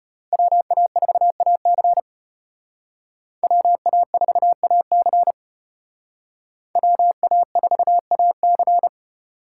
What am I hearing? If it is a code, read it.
Morse code, heard as a keyed tone, about 31 wpm, 700 Hz. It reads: WA4AC WA4AC WA4AC
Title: Speed Racing - USA 2x2 Call Signs 25wpm